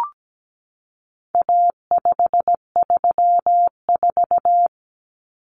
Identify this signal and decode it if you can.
Morse code, heard as a keyed tone, about 17 wpm, 700 Hz. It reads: A534
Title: Common Cut Numbers - 4-Digit 17wpm